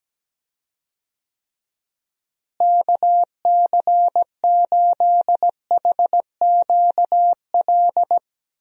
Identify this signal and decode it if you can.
Morse code, heard as a keyed tone, about 17 wpm, 700 Hz. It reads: KC8HQL
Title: USA Call Signs - 6 Characters - Repeated 2x - 17wpm